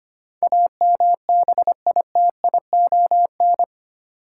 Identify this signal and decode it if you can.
Morse code, heard as a keyed tone, about 25 wpm, 700 Hz. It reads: AMBITION
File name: 8 Letter Words - First 3 Letters Given - 25wpm